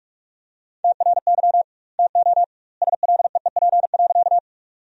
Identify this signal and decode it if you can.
Morse code, heard as a keyed tone, about 46 wpm, 700 Hz. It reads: TRY TO SLEEP.
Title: 2x Word-Spacing - Sentences from Top 500 Words - 45wpm